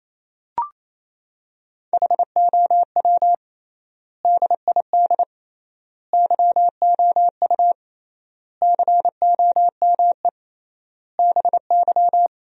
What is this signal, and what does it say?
Morse code, heard as a keyed tone, about 28 wpm, 700 Hz. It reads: HOW DID YOU COME BY
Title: 3x Word-Spacing - Sentences from Top 100 Words - 28wpm